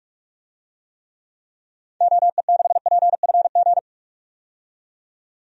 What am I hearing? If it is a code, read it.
Morse code, heard as a keyed tone, about 41 wpm, 700 Hz. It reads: OE6PFG
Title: DXCC-100 Call Signs - 6 Characters - 45wpm